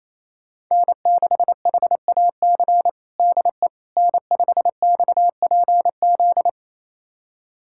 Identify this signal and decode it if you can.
Morse code, heard as a keyed tone, about 27 wpm, 700 Hz. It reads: N6HAC DE N5XPZ
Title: USA Callsign DE Callsign - 5 Characters - 28wpm